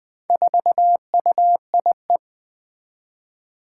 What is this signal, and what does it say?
Morse code, heard as a keyed tone, about 20 wpm, 700 Hz. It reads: 4UIE
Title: Four Challenging Letter-Number Combinations 20wpm